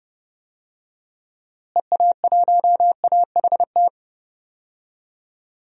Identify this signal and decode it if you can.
Morse code, heard as a keyed tone, about 32 wpm, 700 Hz. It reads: EA1AHT
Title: DXCC-100 Call Signs - 6 Characters - Repeated 2x - 30wpm